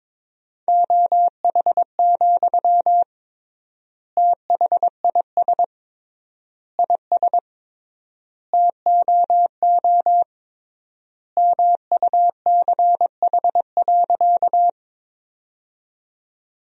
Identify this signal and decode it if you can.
Morse code, heard as a keyed tone, about 22 wpm, 700 Hz. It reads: OH, THIS IS TOO MUCH.
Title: 3x Word-Spacing - Sentences from Top 500 Words - 22wpm